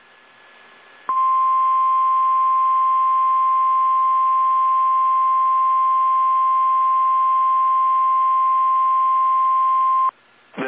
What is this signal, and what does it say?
Emergency Alert System (SAME) tones (the 1050 Hz attention tone). It is heard from 1.1 to 10.1 s.
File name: eas / KSDZKDJL / KSDZKDJL